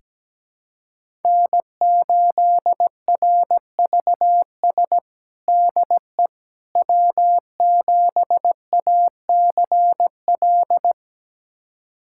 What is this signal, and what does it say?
Morse code, heard as a keyed tone, about 17 wpm, 700 Hz. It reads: N8RVS DE W7ACL
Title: USA Callsign DE Callsign - 5 Characters - 17wpm